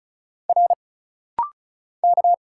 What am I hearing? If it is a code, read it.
Morse code, heard as a keyed tone, about 35 wpm, 700 Hz. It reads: R K
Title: Single Letter-Number Alt - Rapid-Fire 35wpm